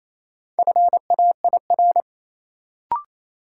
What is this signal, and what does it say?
Morse code, heard as a keyed tone, about 28 wpm, 700 Hz. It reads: FAIR
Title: Sentences from Top 1000 Words - Review Part 5 - New Word Drill - 28wpm